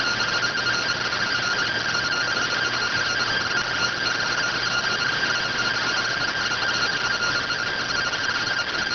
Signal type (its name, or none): two-tone teleprinter signal